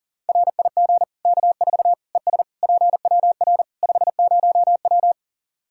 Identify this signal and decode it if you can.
Morse code, heard as a keyed tone, about 37 wpm, 700 Hz. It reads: RIG K4 ES PWR 50W
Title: K1USN SST QSO Practice 40wpm